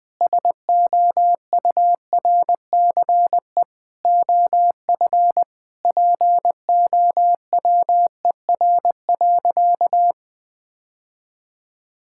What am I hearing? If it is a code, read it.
Morse code, heard as a keyed tone, about 21 wpm, 700 Hz. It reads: SOURCE OF POWER.
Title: Sentences from Top 1000 Words - Review Part 5 - 20wpm